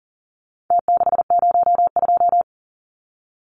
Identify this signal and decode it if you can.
Morse code, heard as a keyed tone, about 40 wpm, 700 Hz. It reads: T602
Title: Numbers - Contest Style - 4-Digit 40wpm